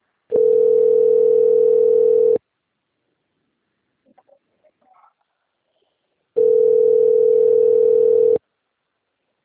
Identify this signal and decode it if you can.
Ringback tone (North America), 440 + 480 Hz.